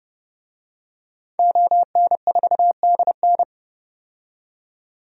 Morse code, heard as a keyed tone, about 30 wpm, 690 Hz. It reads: ON4DN
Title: DXCC-100 Call Signs - 5 Characters - 30wpm